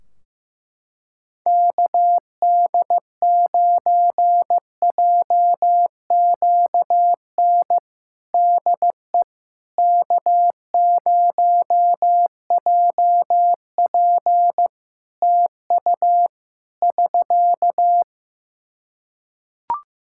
Morse code, heard as a keyed tone, about 15 wpm, 700 Hz. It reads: KD9JQN DE K0JP TU <SK>
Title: Practice QSO - Exchange 4 - Ending - 15wpm